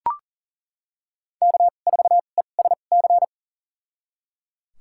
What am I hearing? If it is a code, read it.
Morse code, heard as a keyed tone, about 37 wpm, spaced as for 28 wpm, 700 Hz. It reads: K4ESC
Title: USA Call Signs - 5 Characters - 2x Character Spacing - 40wpm